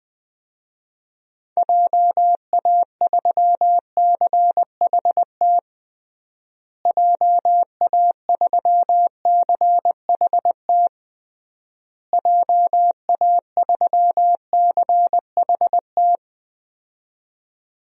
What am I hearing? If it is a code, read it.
Morse code, heard as a keyed tone, about 20 wpm, 700 Hz. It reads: JA3CHT JA3CHT JA3CHT
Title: DXCC-100 Call Signs - 6 Characters - Repeated 3x - 20wpm